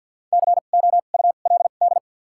Morse code, heard as a keyed tone, about 50 wpm, spaced as for 36 wpm, 700 Hz. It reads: COULD
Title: 2x Character Spacing - Top 100 Words - 50wpm